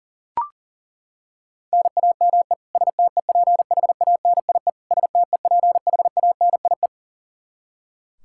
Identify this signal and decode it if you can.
Morse code, heard as a keyed tone, about 37 wpm, 700 Hz. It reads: NAME STEPHANIE STEPHANIE
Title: Practice QSO - Exchange 1 - Beginning - 40wpm